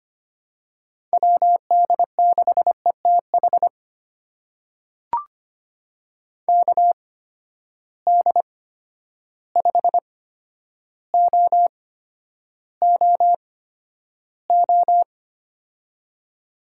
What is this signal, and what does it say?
Morse code, heard as a keyed tone, about 25 wpm, spaced as for 11 wpm, 700 Hz. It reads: WD6ETH KD5OOO
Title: USA Call Signs - 6 Characters - 8x Character Spacing - 25wpm